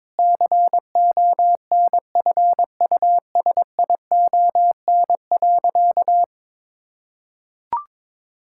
Morse code, heard as a keyed tone, about 22 wpm, 700 Hz. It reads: CONFUSION.
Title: Sentences from Top 1600 Words 22wpm